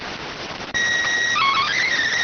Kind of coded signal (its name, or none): SSTV (slow-scan television)